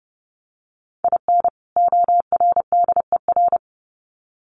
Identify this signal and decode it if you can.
Morse code, heard as a keyed tone, about 30 wpm, 700 Hz. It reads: IN ORDER
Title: Sets of 2 Words 30wpm